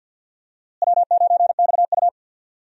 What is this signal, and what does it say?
Morse code, heard as a keyed tone, about 51 wpm, 700 Hz. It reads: W9XU
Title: USA Call Signs - 4 Characters - Repeated 2x - 50wpm